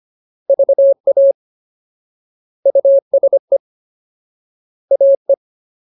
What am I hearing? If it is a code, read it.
Morse code, heard as a keyed tone, about 25 wpm, 550 Hz. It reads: VA USE AE
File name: Ultimate Word List / 25WPM Ultimate